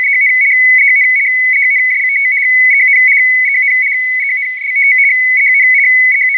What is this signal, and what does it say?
Two-tone teleprinter signal, about 200 Hz shift.